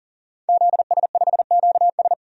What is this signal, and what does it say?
Morse code, heard as a keyed tone, about 39 wpm, 700 Hz. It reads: ZS5QS
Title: DXCC-100 Call Signs - 5 Characters - 40wpm